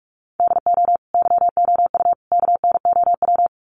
Morse code, heard as a keyed tone, about 44 wpm, 700 Hz. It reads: DO YOU KNOW
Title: Sets of 3 Words 45wpm